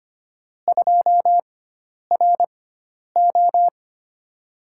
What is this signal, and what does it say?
Morse code, heard as a keyed tone, about 25 wpm, 700 Hz. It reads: 2 R O
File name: Lesson 16g - Letter c - Call Signs 25wpm